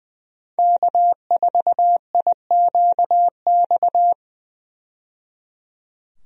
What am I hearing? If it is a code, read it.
Morse code, heard as a keyed tone, about 20 wpm, 700 Hz. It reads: K4IQX
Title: USA Call Signs - 5 Characters - 20wpm